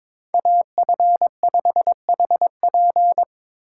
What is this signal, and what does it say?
Morse code, heard as a keyed tone, about 25 wpm, 700 Hz. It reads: AF5HP
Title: Speed Racing - USA Call Signs 22wpm